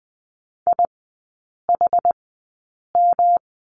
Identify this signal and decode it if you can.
Morse code, heard as a keyed tone, about 20 wpm, spaced as for 8 wpm, 700 Hz. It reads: IHM
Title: 107 Lesson 18e - Letter m - Three Characters 20wpm